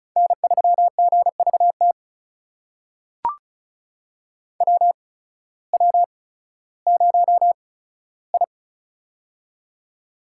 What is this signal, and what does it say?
Morse code, heard as a keyed tone, about 36 wpm, spaced as for 16 wpm, 700 Hz. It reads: N3GVT WW0I
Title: USA Call Signs - 8x Character Spacing - 35wpm